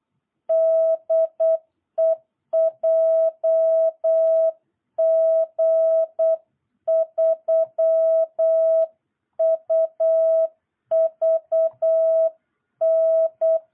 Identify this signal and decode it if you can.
Morse code, heard as a keyed tone, about 8 wpm, 660 Hz. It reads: DE JG3UVN
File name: jh8cbh / 2022cw